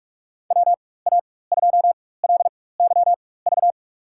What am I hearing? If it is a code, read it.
Morse code, heard as a keyed tone, about 46 wpm, spaced as for 21 wpm, 700 Hz. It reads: WA2LYV
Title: USA Call Signs - 6 Characters - 4x Character Spacing - 45wpm